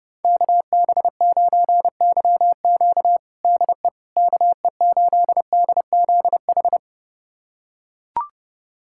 Morse code, heard as a keyed tone, about 30 wpm, 700 Hz. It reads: KB9YQ DE KE8DZH
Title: USA Callsign DE Callsign 30wpm